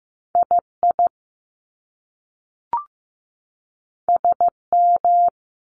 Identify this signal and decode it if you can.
Morse code, heard as a keyed tone, about 15 wpm, 700 Hz. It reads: I I S M
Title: Top Two Letter Combinations in Words 15wpm